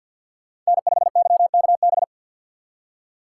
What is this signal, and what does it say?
Morse code, heard as a keyed tone, about 61 wpm, 700 Hz. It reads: N5YKB
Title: Speed Racing - USA Call Signs 50wpm